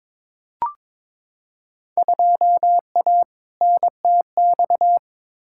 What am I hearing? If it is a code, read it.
Morse code, heard as a keyed tone, about 22 wpm, 700 Hz. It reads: 2A NTX
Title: Field Day Exchanges - 22wpm